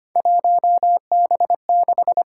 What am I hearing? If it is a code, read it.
Morse code, heard as a keyed tone, about 25 wpm, 700 Hz. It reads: 1B6
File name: Three Letter-Number Combinations - 25wpm